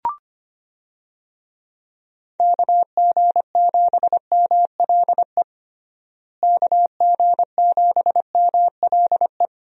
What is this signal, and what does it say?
Morse code, heard as a keyed tone, about 25 wpm, 700 Hz. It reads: KG7MLE KG7MLE
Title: USA Call Signs - 6 Characters - Repeated 2x - 25wpm